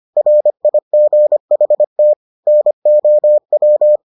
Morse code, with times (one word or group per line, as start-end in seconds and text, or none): 0.2-2.1 RIGHT
2.5-4.0 NOW